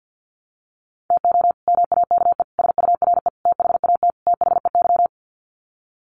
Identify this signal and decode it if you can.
Morse code, heard as a keyed tone, about 50 wpm, 700 Hz. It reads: TO MAKE SURE THAT THEY
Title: Sets of 5 Words Encore2 50wpm